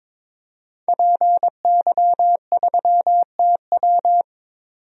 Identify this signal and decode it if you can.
Morse code, heard as a keyed tone, about 21 wpm, 700 Hz. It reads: PY3TW
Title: DXCC-100 Call Signs - 5 Characters - 22wpm